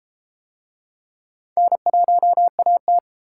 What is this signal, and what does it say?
Morse code, heard as a keyed tone, about 32 wpm, 700 Hz. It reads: N1AT